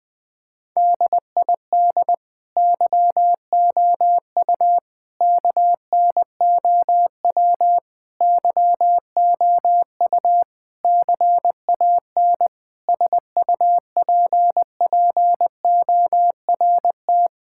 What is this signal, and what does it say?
Morse code, heard as a keyed tone, about 20 wpm, 700 Hz. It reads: DID YOU KNOW YOU CAN SUPPORT
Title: Sets of 9 Words - 20wpm